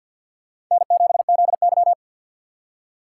Morse code, heard as a keyed tone, about 53 wpm, 690 Hz. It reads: N7ZY
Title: USA Call Signs - 2x Character Spacing - 50wpm